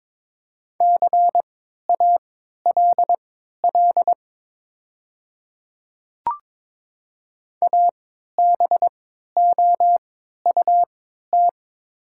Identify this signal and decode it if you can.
Morse code, heard as a keyed tone, about 22 wpm, spaced as for 13 wpm, 700 Hz. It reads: CALL ABOUT
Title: 3x Character Spacing - Top 500 Words - 22wpm